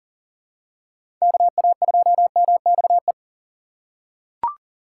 Morse code, heard as a keyed tone, about 40 wpm, 700 Hz. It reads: KA2MXE
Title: USA Call Signs - 6 Characters - 3x Character Spacing - 40wpm